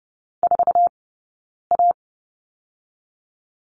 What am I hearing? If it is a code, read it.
Morse code, heard as a keyed tone, about 30 wpm, 700 Hz. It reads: 4 A
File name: course / 30:10wpm / 099 Lesson 17d - Letter u - Two Characters 30wpm